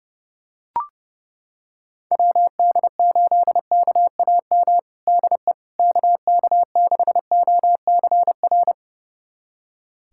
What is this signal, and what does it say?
Morse code, heard as a keyed tone, about 30 wpm, 700 Hz. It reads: WD8KAM DE KK6OCR
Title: USA Callsign DE Callsign 30wpm